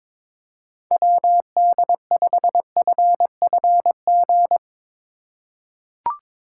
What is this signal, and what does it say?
Morse code, heard as a keyed tone, about 22 wpm, 700 Hz. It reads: WD5FFG
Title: USA Call Signs - 6 Characters - 4x Character Spacing - 22wpm